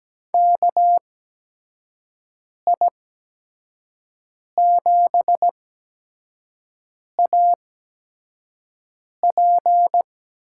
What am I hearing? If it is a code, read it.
Morse code, heard as a keyed tone, about 17 wpm, spaced as for 5 wpm, 700 Hz. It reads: KI7AP
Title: USA Call Signs - 6 Characters - 8x Character Spacing - 17wpm